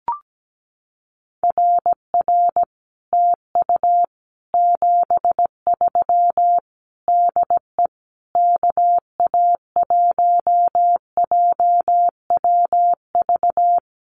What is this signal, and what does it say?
Morse code, heard as a keyed tone, about 17 wpm, 700 Hz. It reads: RR TU 73 DE KA1JWV
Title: Common QSO Phrases Encore 17wpm